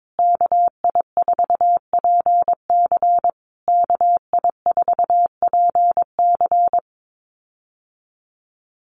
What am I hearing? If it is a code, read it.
Morse code, heard as a keyed tone, about 22 wpm, 700 Hz. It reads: KI4PC KI4PC
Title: USA Call Signs Repeated 3x 22wpm